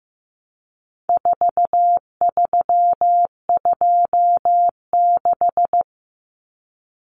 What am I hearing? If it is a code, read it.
Morse code, heard as a keyed tone, about 15 wpm, 700 Hz. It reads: 4326
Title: Common Cut Numbers - 4-Digit 15wpm